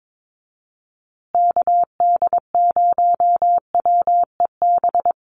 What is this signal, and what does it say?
Morse code, heard as a keyed tone, about 22 wpm, 700 Hz. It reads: KD0WEB
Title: USA Call Signs Repeated 2x 22wpm